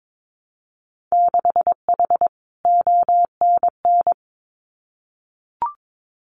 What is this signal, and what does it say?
Morse code, heard as a keyed tone, about 22 wpm, 700 Hz. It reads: 6H ONN
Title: Winter Field Day Exchanges Encore 22wpm